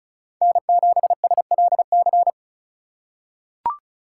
Morse code, heard as a keyed tone, about 34 wpm, 700 Hz. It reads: N7SLC
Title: USA Call Signs - 5 Characters - 4x Character Spacing - 35wpm